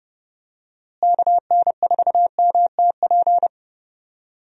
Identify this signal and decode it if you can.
Morse code, heard as a keyed tone, about 30 wpm, 700 Hz. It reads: KN4MTP
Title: USA Call Signs - 6 Characters - Repeated 3x - 30wpm